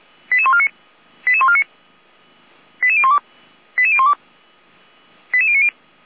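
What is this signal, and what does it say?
Five-tone selective call (ZVEI), heard as quick runs of five tones, tones of about 70 ms. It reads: 88149 88149 88012 88012 88090
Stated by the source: ZVEI five-tone call received via SDR, fire brigade Orlamünde (feuerwehr-orlamuende) receiver.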